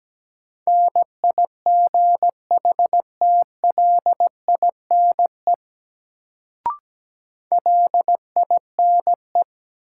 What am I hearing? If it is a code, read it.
Morse code, heard as a keyed tone, about 17 wpm, 700 Hz. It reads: NIGHTLINE LINE
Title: Sets of 5 Words - New Word Drill - 17wpm